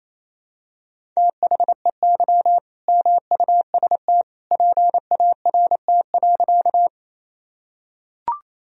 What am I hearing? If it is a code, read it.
Morse code, heard as a keyed tone, about 28 wpm, 700 Hz. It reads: THEY MUST PART.
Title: Sentences from Top 1000 Words - Review Part 1 - 28wpm